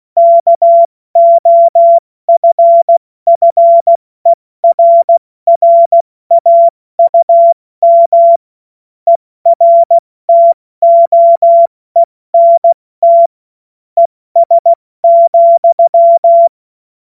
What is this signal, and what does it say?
Morse code, heard as a keyed tone, about 16 wpm, spaced as for 14 wpm, 680 Hz. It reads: KOFFERRAUM ERTOENT ES,
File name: Morsekurs / Witze / Witz 28. Dezember 2021, 70 F80